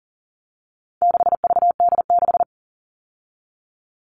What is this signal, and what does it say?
Morse code, heard as a keyed tone, about 40 wpm, 700 Hz. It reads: 6VD6
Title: All Cut Numbers - 4-Digit 40wpm